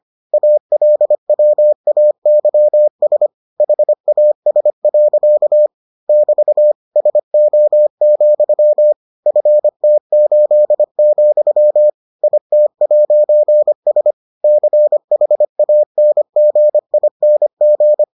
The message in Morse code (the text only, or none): ALWAYS HAS. <BT> SO, FT8, IT'S CHANGING